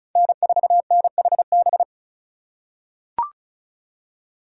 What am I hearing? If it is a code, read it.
Morse code, heard as a keyed tone, about 36 wpm, 700 Hz. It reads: N4NHB
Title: USA Call Signs - 5 Characters - 6x Character Spacing - 35wpm